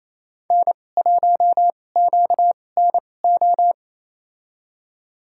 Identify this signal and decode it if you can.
Morse code, heard as a keyed tone, about 27 wpm, spaced as for 20 wpm, 700 Hz. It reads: N1QNO
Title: USA Call Signs - 2x Character Spacing - 28wpm